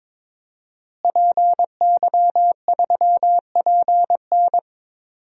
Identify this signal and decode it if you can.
Morse code, heard as a keyed tone, about 22 wpm, 700 Hz. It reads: PY3PN
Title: DXCC-100 Call Signs - 5 Characters - Repeated 2x - 22wpm